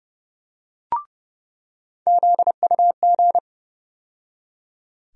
Morse code, heard as a keyed tone, about 31 wpm, 700 Hz. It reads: ZUG
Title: USA Call Sign Postfixes - 3 Letters 30wpm